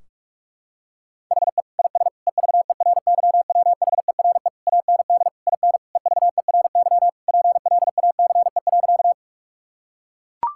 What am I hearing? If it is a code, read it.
Morse code, heard as a keyed tone, about 42 wpm, 700 Hz. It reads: HE IS EVERYWHERE AND IN EVERY PLACE.
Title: Sentences from Top 1600 Words 45wpm